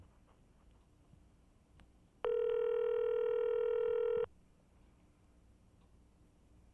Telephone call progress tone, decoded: ringback tone